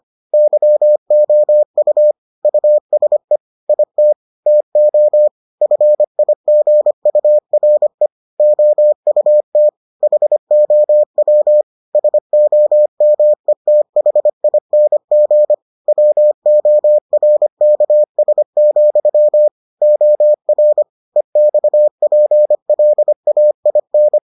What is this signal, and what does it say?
Morse code, heard as a keyed tone, about 25 wpm, 600 Hz. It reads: YOU USE IT TO FIGURE OUT HOW SOMETHING WORKS, OR EXPLAIN